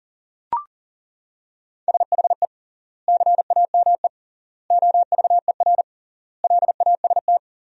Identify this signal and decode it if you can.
Morse code, heard as a keyed tone, about 37 wpm, 700 Hz. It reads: SHE CAME OVER LAST
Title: 3x Word-Spacing - Sentences from Top 300 Words - 40wpm